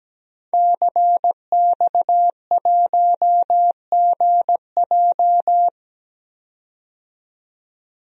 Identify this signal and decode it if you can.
Morse code, heard as a keyed tone, about 17 wpm, 700 Hz. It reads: CX1GJ
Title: DXCC-100 Call Signs - 5 Characters - 17wpm